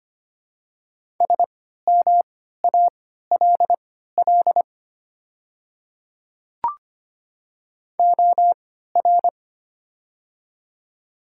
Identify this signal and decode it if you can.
Morse code, heard as a keyed tone, about 25 wpm, spaced as for 14 wpm, 700 Hz. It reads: SMALL OR
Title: 3x Character Spacing - Top 200 Words - 25wpm